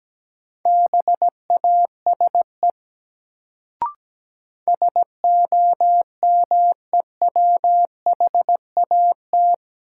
Morse code, heard as a keyed tone, about 17 wpm, 700 Hz. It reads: BASE SOMEWHAT
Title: Sentences from Top 1000 Words - Review Part 1 - New Word Drill - 17wpm